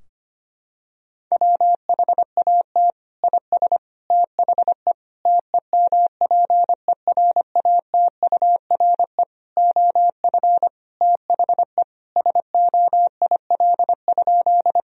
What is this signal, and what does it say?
Morse code, heard as a keyed tone, about 25 wpm, 700 Hz. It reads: WHAT IS THE TEMPERATURE OF THE SOIL?
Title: Sentences from Top 1000 Words - Review Part 4 - 25wpm